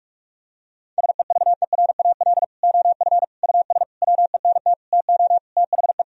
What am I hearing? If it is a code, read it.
Morse code, heard as a keyed tone, about 46 wpm, 700 Hz. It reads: SEVERAL OF US WENT TO THE
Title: Sentences from Top 1000 Words - Review Part 2 - 45wpm